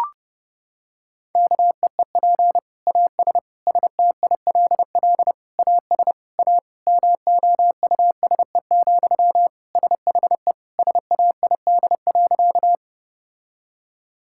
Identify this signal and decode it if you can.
Morse code, heard as a keyed tone, about 30 wpm, 700 Hz. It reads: KEEP AS STILL AS A MOUSE, SHE SAID.
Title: Sentences from Top 1000 Words - Review Part 5 - 30wpm